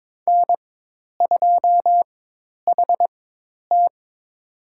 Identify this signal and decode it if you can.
Morse code, heard as a keyed tone, about 21 wpm, spaced as for 10 wpm, 700 Hz. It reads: N2HT
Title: USA Call Signs - 4 Characters - 4x Character Spacing - 22wpm